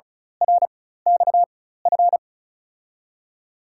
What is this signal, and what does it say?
Morse code, heard as a keyed tone, about 36 wpm, spaced as for 16 wpm, 700 Hz. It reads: RXF
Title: USA Call Signs - 4x Character Spacing - 35wpm